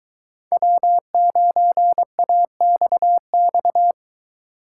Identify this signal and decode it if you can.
Morse code, heard as a keyed tone, about 22 wpm, 700 Hz. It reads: W9AXX